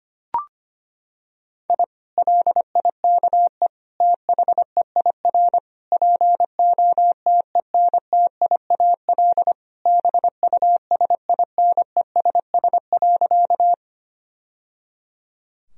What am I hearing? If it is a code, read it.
Morse code, heard as a keyed tone, about 25 wpm, 700 Hz. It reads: I LIKE THEIR POTENTIAL BUSINESS.
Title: Sentences from Top 1000 Words - Review Part 4 - 25wpm